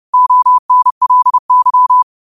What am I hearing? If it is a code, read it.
Morse code, heard as a keyed tone, about 30 wpm, 1000 Hz. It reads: ONLY
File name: qotdfiles / 30wpm